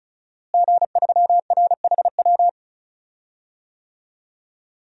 Morse code, heard as a keyed tone, about 36 wpm, 700 Hz. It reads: G3RHW
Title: DXCC-100 Call Signs - 5 Characters - Repeated 3x - 35wpm